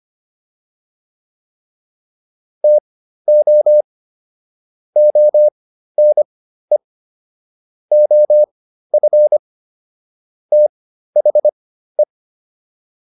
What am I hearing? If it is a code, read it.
Morse code, heard as a keyed tone, about 25 wpm, spaced as for 13 wpm, 600 Hz. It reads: TO ONE OF THE